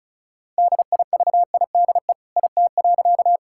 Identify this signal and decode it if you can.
Morse code, heard as a keyed tone, about 34 wpm, 700 Hz. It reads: DIVIDE IT.
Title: Sentences from Top 1600 Words 35wpm